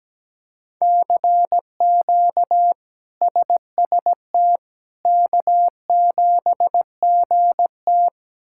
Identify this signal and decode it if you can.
Morse code, heard as a keyed tone, about 17 wpm, 700 Hz. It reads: CQ SST K7GT
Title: K1USN SST QSO Practice - 17wpm